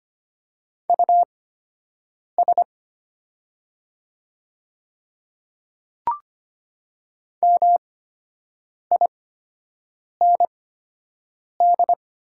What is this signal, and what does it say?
Morse code, heard as a keyed tone, about 25 wpm, spaced as for 7 wpm, 700 Hz. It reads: US MIND
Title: 8x Character Spacing - Top 500 Words - 25wpm